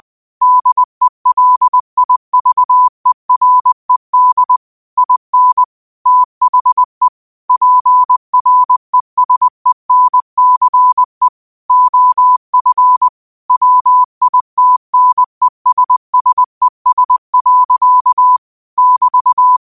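Morse code, heard as a keyed tone, about 20 wpm, 1000 Hz. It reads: DELIVERED IN THE PRESENCE OF WITNESSES. <BT>
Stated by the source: Morse quote practice at 20 wpm.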